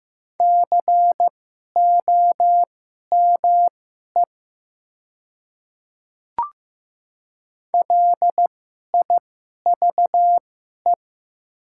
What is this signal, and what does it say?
Morse code, heard as a keyed tone, about 15 wpm, spaced as for 11 wpm, 690 Hz. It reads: COME LIVE
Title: 2x Character Spacing - Top 200 Words - 15wpm